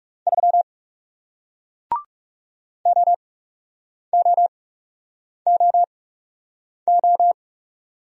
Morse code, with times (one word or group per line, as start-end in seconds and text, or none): 0.3-0.6 3
2.9-7.3 OOOO